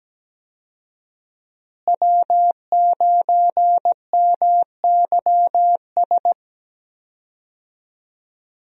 Morse code, heard as a keyed tone, about 17 wpm, 700 Hz. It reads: W9MYS
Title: USA Call Signs - 5 Characters - Repeated 2x - 17wpm